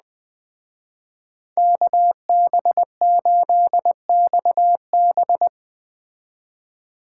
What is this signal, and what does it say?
Morse code, heard as a keyed tone, about 20 wpm, 700 Hz. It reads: KB8XB
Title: Speed Racing - USA Call Signs 17wpm